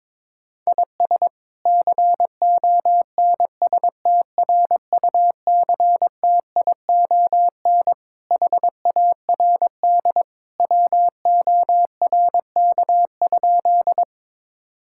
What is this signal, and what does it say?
Morse code, heard as a keyed tone, about 22 wpm, 700 Hz. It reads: IS CONSTRUCTION HARD WORK?
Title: Sentences from Top 1600 Words 22wpm